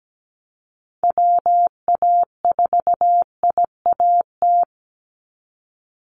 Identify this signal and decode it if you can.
Morse code, heard as a keyed tone, about 17 wpm, 700 Hz. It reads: WA4IAT
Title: USA Call Signs 17wpm